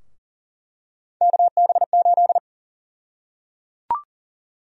Morse code, heard as a keyed tone, about 37 wpm, 700 Hz. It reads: KB8
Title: USA Call Sign Prefixes - 40wpm